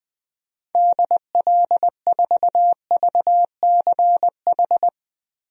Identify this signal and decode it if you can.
Morse code, heard as a keyed tone, about 21 wpm, 700 Hz. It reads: DL4VCH
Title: DXCC-100 Call Signs - 6 Characters - Repeated 2x - 20wpm